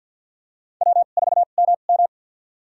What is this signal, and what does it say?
Morse code, heard as a keyed tone, about 53 wpm, spaced as for 37 wpm, 700 Hz. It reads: W4MM
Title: USA Call Signs - 2x Character Spacing - 50wpm